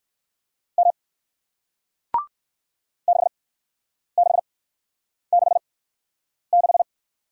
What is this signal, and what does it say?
Morse code, heard as a keyed tone, about 54 wpm, spaced as for 9 wpm, 700 Hz. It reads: M 6666
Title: Speed-Racing Single Letter-Number 45wpm